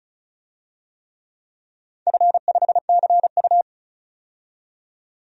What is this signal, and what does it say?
Morse code, heard as a keyed tone, about 36 wpm, 700 Hz. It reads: F5CU
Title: DXCC-100 Call Signs - 4 Characters - 35wpm